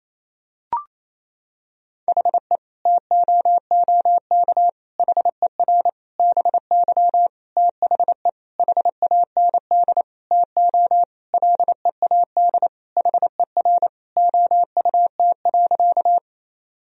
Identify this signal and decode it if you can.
Morse code, heard as a keyed tone, about 28 wpm, 700 Hz. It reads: HE TOOK HER BY THE HAND TO LEAD HER OUT.
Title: Sentences from Top 1000 Words - Review Part 2 - 28wpm